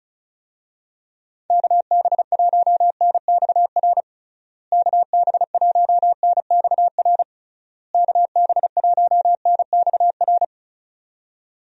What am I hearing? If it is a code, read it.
Morse code, heard as a keyed tone, about 34 wpm, 700 Hz. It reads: KB1NXR KB1NXR KB1NXR
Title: USA Call Signs - 6 Characters - Repeated 3x - 35wpm